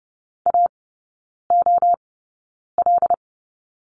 Morse code, heard as a keyed tone, about 30 wpm, spaced as for 9 wpm, 690 Hz. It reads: AOL
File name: Lesson 16e - Letter c - Three Characters 30wpm